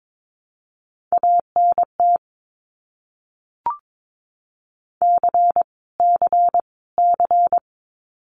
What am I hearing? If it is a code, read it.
Morse code, heard as a keyed tone, about 22 wpm, 700 Hz. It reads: ANT C C C
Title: Abbreviations Repeated 3x - 22wpm